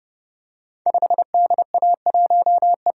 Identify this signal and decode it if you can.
Morse code, heard as a keyed tone, about 30 wpm, 700 Hz. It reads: 5DA1E
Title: Code Groups - 5 Letter:Numbers - 30wpm